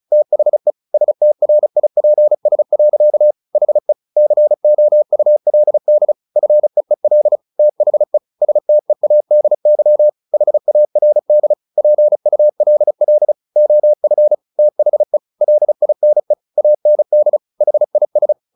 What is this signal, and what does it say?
Morse code, heard as a keyed tone, about 34 wpm, 600 Hz. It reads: THE STRIPS. HE COULD FEEL THE STEADY HARD PULL OF THE LINE AND HIS
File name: the old man and the sea / the old man and the sea 1 35wpm